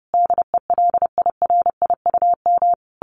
Morse code, heard as a keyed tone, about 30 wpm, 700 Hz. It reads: DELIRIUM
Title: A Sound Of Thunder 30wpm